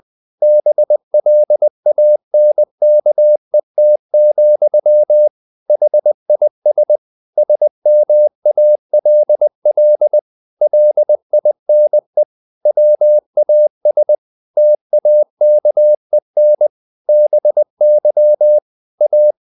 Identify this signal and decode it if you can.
Morse code, heard as a keyed tone, about 20 wpm, 600 Hz. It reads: BLANKET, HIS SMALL LINE WAS TAKEN BY A
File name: the old man and the sea part 2 20wpm